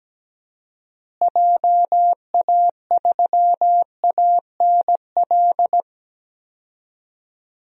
Morse code, heard as a keyed tone, about 17 wpm, 700 Hz. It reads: JA3ANL
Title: DXCC-100 Call Signs - 6 Characters - Repeated 2x - 17wpm